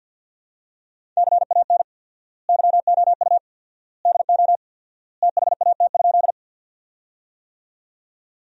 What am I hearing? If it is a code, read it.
Morse code, heard as a keyed tone, about 52 wpm, 700 Hz. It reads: CAN YOU DO THAT?
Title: 4x Word-Spacing - Sentences from Top 100 Words - 50wpm